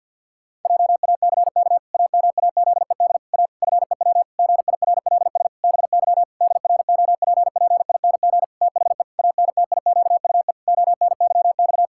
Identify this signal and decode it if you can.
Morse code, heard as a keyed tone, about 47 wpm, 700 Hz. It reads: JACK AMAZED A FEW GIRLS BY DROPPING THE ANTIQUE ONYX
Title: Pangram Challenge v2 - 50wpm